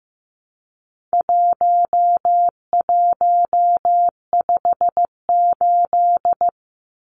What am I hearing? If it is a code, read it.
Morse code, heard as a keyed tone, about 15 wpm, 700 Hz. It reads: 1158
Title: Numbers - 4-Digit 15wpm